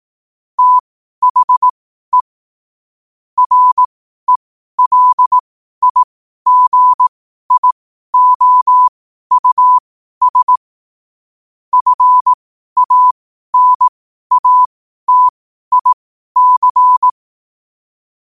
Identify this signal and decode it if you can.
Morse code, heard as a keyed tone, about 18 wpm, spaced as for 12 wpm, 1000 Hz. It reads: THE RELIGIOUS FANATIC
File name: qotdfiles / 12wpm